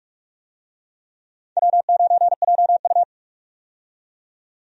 Morse code, heard as a keyed tone, about 46 wpm, 700 Hz. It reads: W9JU
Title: USA Call Signs - 4 Characters - 45wpm